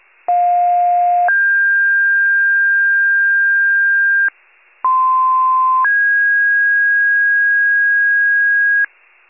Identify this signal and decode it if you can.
Two-tone page, heard as a one-second tone, then a three-second tone, about 710 then 1600 Hz, about 1040 then 1650 Hz. It is heard from 0.3 to 8.9 s.
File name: pager / 95 RESCUE DUTY / wcalert